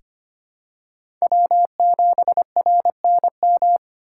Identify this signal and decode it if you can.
Morse code, heard as a keyed tone, about 25 wpm, 700 Hz. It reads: W7RNM
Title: USA Call Signs - 3x Character Spacing - 25wpm